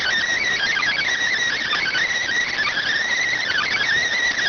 Emergency Alert System (SAME) tones, heard as harsh buzzing bursts. It is heard from 0.4 to 3.8 s.